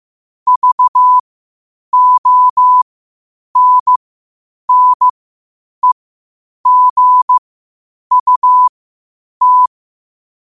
Morse code, heard as a keyed tone, about 15 wpm, spaced as for 8 wpm, 1000 Hz. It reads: VONNEGUT
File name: qotdfiles / 8wpm